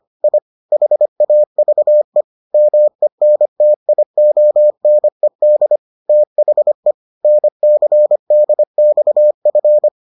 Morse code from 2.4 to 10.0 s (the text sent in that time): MENTIONED THE NCDXF